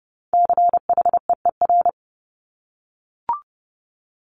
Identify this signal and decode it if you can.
Morse code, heard as a keyed tone, about 30 wpm, 700 Hz. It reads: CHEER
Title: Multi-letter Phonemes Repeated 3x 30wpm